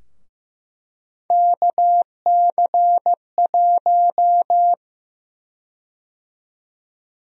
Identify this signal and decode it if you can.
Morse code, heard as a keyed tone, about 16 wpm, 700 Hz. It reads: KC1
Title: USA Call Sign Prefixes - Repeated 2x - 15wpm